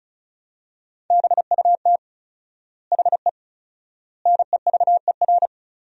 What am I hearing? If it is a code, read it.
Morse code, heard as a keyed tone, about 36 wpm, 700 Hz. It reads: BUT HE NEVER
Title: 4x Word-Spacing - Sentences from Top 400 Words - 35wpm